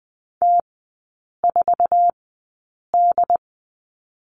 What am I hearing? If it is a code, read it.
Morse code, heard as a keyed tone, about 20 wpm, spaced as for 8 wpm, 700 Hz. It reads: T4D
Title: 067 Lesson 12e - Letter d - Three Characters 20wpm